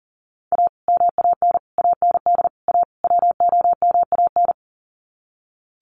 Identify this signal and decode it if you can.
Morse code, heard as a keyed tone, about 40 wpm, 700 Hz. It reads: A MAN AND A WOMAN
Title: Sets of 5 Words Encore 40wpm